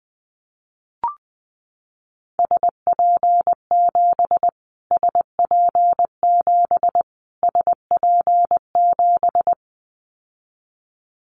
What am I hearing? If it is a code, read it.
Morse code, heard as a keyed tone, about 20 wpm, 700 Hz. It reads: SP7 SP7 SP7
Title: Callsign Prefixes for Top 25 DXCC - Repeated 3x - 20wpm